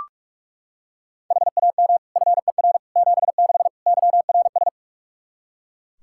Morse code, heard as a keyed tone, about 41 wpm, 700 Hz. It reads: HAM FER 76 YRS
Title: Practice QSO - Exchange 3 - More Info - 45wpm